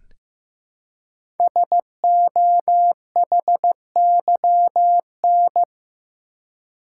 Morse code, heard as a keyed tone, about 15 wpm, 700 Hz. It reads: SOHYN
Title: Code Groups - 5 Letters - 15wpm